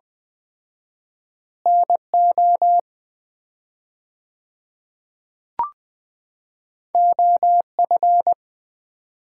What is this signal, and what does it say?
Morse code, heard as a keyed tone, about 20 wpm, 700 Hz. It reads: NO OF
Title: 4x Word-Spacing - Sets of 3 Words - 20wpm